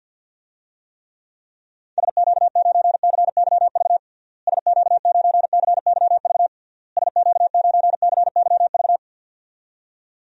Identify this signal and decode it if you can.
Morse code, heard as a keyed tone, about 52 wpm, 700 Hz. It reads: SQ9CYV SQ9CYV SQ9CYV
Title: DXCC-100 Call Signs - 6 Characters - Repeated 3x - 50wpm